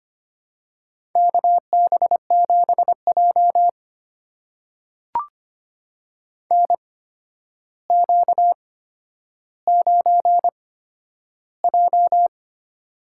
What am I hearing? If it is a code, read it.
Morse code, heard as a keyed tone, about 25 wpm, spaced as for 11 wpm, 700 Hz. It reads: KB7J NQ9J
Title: USA Call Signs - 4 Characters - 8x Character Spacing - 25wpm